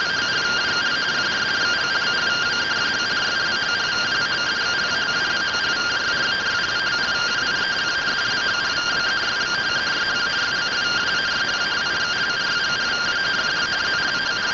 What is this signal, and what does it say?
Two-tone teleprinter signal, about 170 Hz shift.